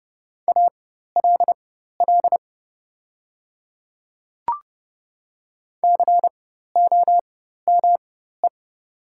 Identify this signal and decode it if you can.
Morse code, heard as a keyed tone, about 30 wpm, spaced as for 14 wpm, 700 Hz. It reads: ALL COME
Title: 4x Character Spacing - Top 100 Words - 30wpm